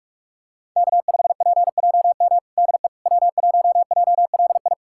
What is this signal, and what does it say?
Morse code, heard as a keyed tone, about 46 wpm, 700 Hz. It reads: K5PJM DE W1JLI
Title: USA Callsign DE Callsign - 5 Characters - 45wpm